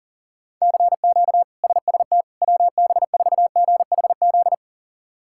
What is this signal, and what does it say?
Morse code, heard as a keyed tone, about 37 wpm, 700 Hz. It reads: CQ SST WB4GHZ
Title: K1USN SST QSO Practice - 40wpm